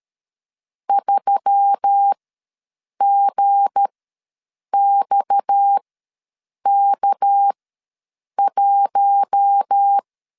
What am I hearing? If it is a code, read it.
Morse code, heard as a keyed tone, about 13 wpm, spaced as for 7 wpm, 800 Hz. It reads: 3GXK1
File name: LEZIONI / 7 GF9Z / 7 G ws 12wpm 3140